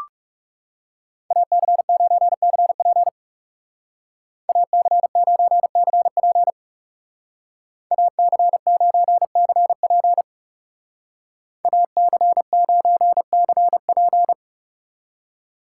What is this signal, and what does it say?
Morse code, heard as a keyed tone, about 35 wpm, 700 Hz. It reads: AC9CP AC9CP AC9CP AC9CP
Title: Speed Racing - USA Call Signs 30wpm